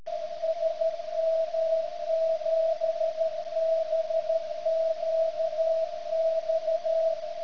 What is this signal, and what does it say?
Morse code, heard as a keyed tone, about 13 wpm, 650 Hz. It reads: SM7BQX
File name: ft8-vs-cw / 75-to-150 CW 100hz